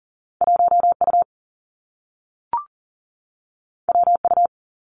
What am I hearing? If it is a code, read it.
Morse code, heard as a keyed tone, about 40 wpm, 700 Hz. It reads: 1U WU
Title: Two Challenging Letter-Number Combinations 40wpm